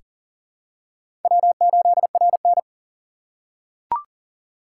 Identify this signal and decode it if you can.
Morse code, heard as a keyed tone, about 40 wpm, 700 Hz. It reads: W8RN
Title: USA Call Signs - 4 Characters - 4x Character Spacing - 40wpm